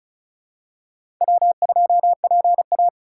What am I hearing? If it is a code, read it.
Morse code, heard as a keyed tone, about 36 wpm, 700 Hz. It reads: W2PA
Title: USA Call Signs - 4 Characters - 4x Character Spacing - 35wpm